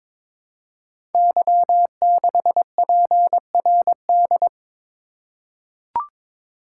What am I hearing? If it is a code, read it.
Morse code, heard as a keyed tone, about 22 wpm, 700 Hz. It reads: Y6PRD
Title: Code Groups - 5 Letter:Numbers - 22wpm